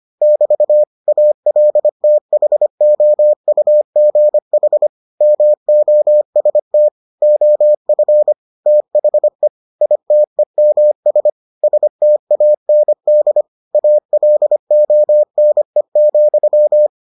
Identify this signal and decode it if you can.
Morse code, heard as a keyed tone, about 25 wpm, 600 Hz. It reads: <BT> ALTHOUGH MOST OF THE ITEMS STAND ALONE,